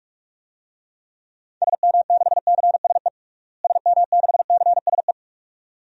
Morse code, heard as a keyed tone, about 46 wpm, 700 Hz. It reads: SM6CSE SM6CSE
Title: DXCC-100 Call Signs - 6 Characters - Repeated 2x - 45wpm